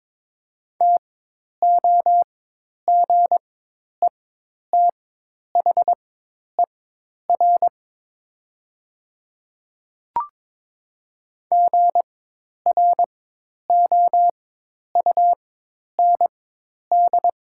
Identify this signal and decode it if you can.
Morse code, heard as a keyed tone, about 22 wpm, spaced as for 10 wpm, 700 Hz. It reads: TOGETHER GROUND
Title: 4x Character Spacing - Top 500 Words - 22wpm